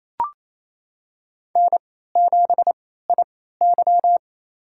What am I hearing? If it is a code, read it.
Morse code, heard as a keyed tone, about 28 wpm, spaced as for 16 wpm, 700 Hz. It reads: N7IY
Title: USA Call Signs - 4 Characters - 3x Character Spacing - 28wpm